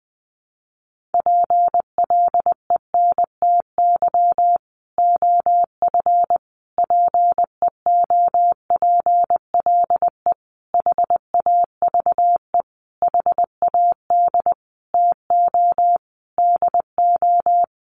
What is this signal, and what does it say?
Morse code, heard as a keyed tone, about 19 wpm, 700 Hz. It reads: PLENTY OF PEOPLE HAVE HAD TO DO
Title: Sentences from Top 1300 Words 20wpm